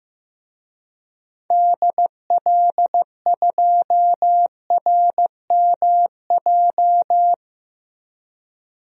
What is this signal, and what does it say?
Morse code, heard as a keyed tone, about 15 wpm, 700 Hz. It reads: DL2RMJ
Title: DXCC-100 Call Signs - 6 Characters - 15wpm